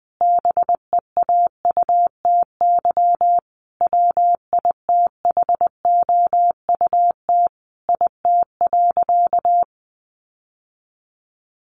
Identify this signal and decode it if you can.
Morse code, heard as a keyed tone, about 20 wpm, 700 Hz. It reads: BEAUTY WITHOUT IT.
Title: Sentences from Top 600 Words 20wpm